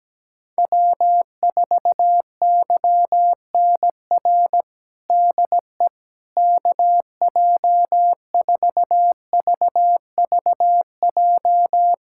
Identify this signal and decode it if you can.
Morse code, heard as a keyed tone, about 17 wpm, 700 Hz. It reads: W4YNR DE KJ4VVJ
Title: USA Callsign DE Callsign 17wpm